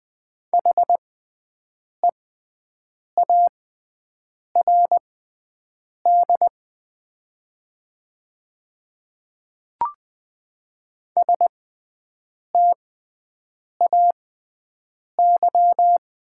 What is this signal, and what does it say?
Morse code, heard as a keyed tone, about 20 wpm, spaced as for 7 wpm, 700 Hz. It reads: HEARD STAY
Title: 6x Character Spacing - Top 500 Words - 20wpm